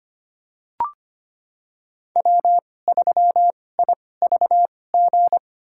Morse code, heard as a keyed tone, about 25 wpm, spaced as for 18 wpm, 700 Hz. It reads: W3IVG
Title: USA Call Signs - 5 Characters - 2x Character Spacing - 25wpm